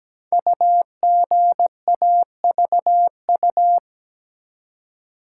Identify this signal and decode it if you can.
Morse code, heard as a keyed tone, about 17 wpm, 700 Hz. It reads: UGAVU
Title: Code Groups - 5 Letter:Numbers - 17wpm